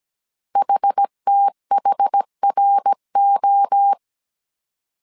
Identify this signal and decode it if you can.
Morse code, heard as a keyed tone, about 17 wpm, 800 Hz. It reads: HTHRO